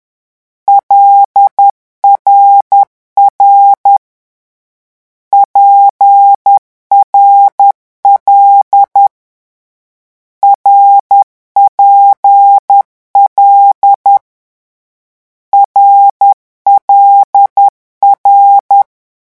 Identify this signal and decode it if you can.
Morse code, heard as a keyed tone, about 10 wpm, 800 Hz. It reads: LRR PRL RPL RLR
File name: morse mp3 / lecon6-vitesse 10